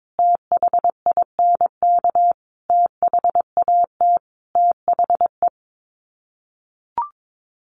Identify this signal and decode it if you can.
Morse code, heard as a keyed tone, about 22 wpm, 700 Hz. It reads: THINK THAT THE
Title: Sets of 5 Words Encore 22wpm